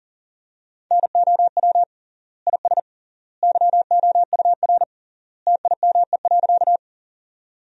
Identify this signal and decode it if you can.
Morse code, heard as a keyed tone, about 37 wpm, 700 Hz. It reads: NOW IS YOUR TIME.
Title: 3x Word-Spacing - Sentences from Top 100 Words - 40wpm